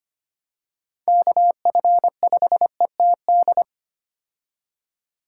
Morse code, heard as a keyed tone, about 25 wpm, 700 Hz. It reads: KF5ETD
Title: K1USN SST QSO Practice - 25wpm